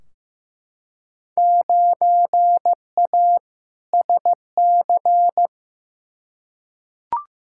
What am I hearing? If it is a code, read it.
Morse code, heard as a keyed tone, about 15 wpm, 700 Hz. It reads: 9A SC
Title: Field Day Exchanges Encore - 15wpm